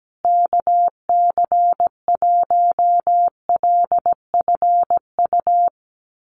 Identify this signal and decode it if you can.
Morse code, heard as a keyed tone, about 17 wpm, 700 Hz. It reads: KC1LFU
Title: USA Call Signs Repeated 2x 17wpm